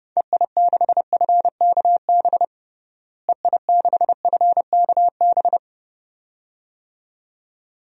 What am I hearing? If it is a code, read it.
Morse code, heard as a keyed tone, about 30 wpm, 700 Hz. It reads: EI6FKB EI6FKB
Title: DXCC-100 Call Signs - 6 Characters - Repeated 2x - 30wpm